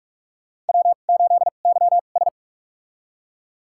Morse code, heard as a keyed tone, about 43 wpm, spaced as for 32 wpm, 700 Hz. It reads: W8YS
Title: USA Call Signs - 4 Characters - 2x Character Spacing - 45wpm